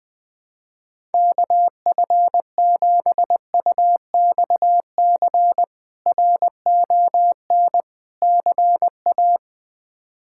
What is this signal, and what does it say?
Morse code, heard as a keyed tone, about 20 wpm, 700 Hz. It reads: KF7UXC RON CA
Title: K1USN SST QSO Practice - 20wpm